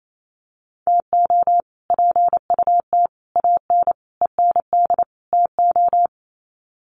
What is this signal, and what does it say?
Morse code, heard as a keyed tone, about 28 wpm, 700 Hz. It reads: TO PUT AN END TO
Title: Sets of 5 Words Encore2 28wpm